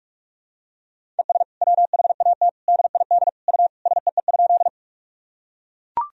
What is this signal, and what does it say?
Morse code, heard as a keyed tone, about 43 wpm, 700 Hz. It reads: ES WHAT DID U SEE?
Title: Sentences from Top 100 Words and Abbreviations 45wpm